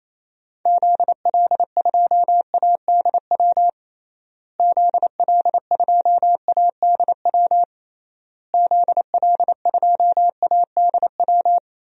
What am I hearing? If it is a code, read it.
Morse code, heard as a keyed tone, about 28 wpm, 700 Hz. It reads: ZL2ADW ZL2ADW ZL2ADW
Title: DXCC-100 Call Signs - 6 Characters - Repeated 3x - 28wpm